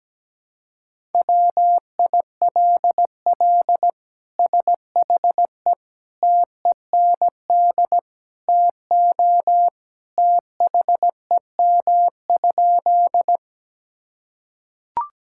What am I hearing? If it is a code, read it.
Morse code, heard as a keyed tone, about 17 wpm, 700 Hz. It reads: WILL SHE TEND TO THEM?
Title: Sentences from Top 1000 Words - Review Part 5 - 17wpm